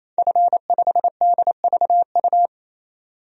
Morse code, heard as a keyed tone, about 28 wpm, 700 Hz. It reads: F5DVU
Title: DXCC-100 Call Signs - 5 Characters - 28wpm